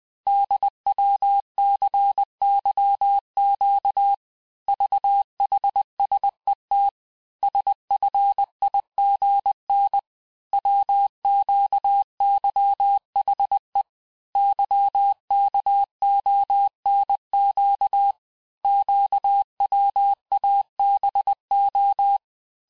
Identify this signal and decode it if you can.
Morse code, heard as a keyed tone, about 20 wpm, 800 Hz. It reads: DWCYQ VHSET SFIGN WQYHE YKONQ QWABO
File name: cours-858-texte-chiffre-vitesse-19mpm